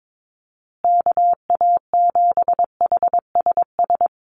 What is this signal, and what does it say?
Morse code, heard as a keyed tone, about 22 wpm, 700 Hz. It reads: KA7HSS
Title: USA Call Signs Repeated 2x 22wpm